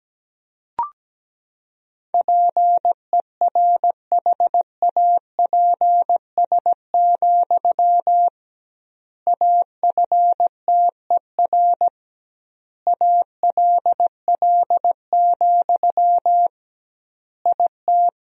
Morse code, heard as a keyed tone, about 17 wpm, 700 Hz. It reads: PERHAPS, AFTER ALL, IT
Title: 2x Word-Spacing - Sentences from Top 500 Words - 17wpm